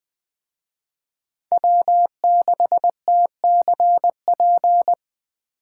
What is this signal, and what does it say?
Morse code, heard as a keyed tone, about 21 wpm, 700 Hz. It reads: W6TCP